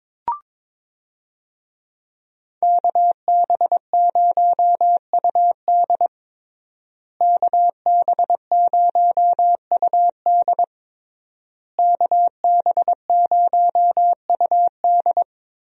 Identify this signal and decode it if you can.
Morse code, heard as a keyed tone, about 22 wpm, 700 Hz. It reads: KB0UD KB0UD KB0UD
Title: USA Call Signs - 5 Characters - Repeated 3x - 22wpm